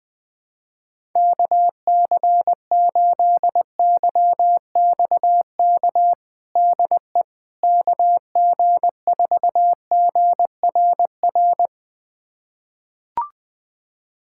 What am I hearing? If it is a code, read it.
Morse code, heard as a keyed tone, about 20 wpm, 700 Hz. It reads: KC8YXK DE KG4GRR
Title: USA Callsign DE Callsign - 6 Characters - 20wpm